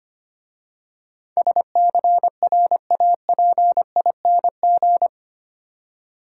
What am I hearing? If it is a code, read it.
Morse code, heard as a keyed tone, about 25 wpm, 700 Hz. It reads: SCRAPING
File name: Easy Words - Speed-Racing with Hints - 25wpm